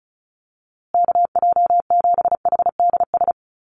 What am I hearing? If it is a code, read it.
Morse code, heard as a keyed tone, about 34 wpm, 700 Hz. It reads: KJ7HDS
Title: USA Call Signs Repeated 3x 35wpm